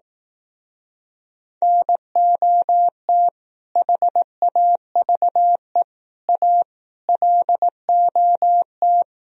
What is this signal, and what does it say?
Morse code, heard as a keyed tone, about 18 wpm, 700 Hz. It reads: NOT HAVE A LOT
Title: Speed-Racing - Sets of 5 Words 15wpm